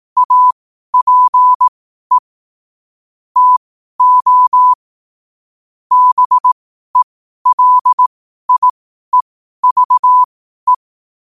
Morse code, heard as a keyed tone, about 18 wpm, spaced as for 12 wpm, 1000 Hz. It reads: APE TO BELIEVE
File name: qotdfiles / 12wpm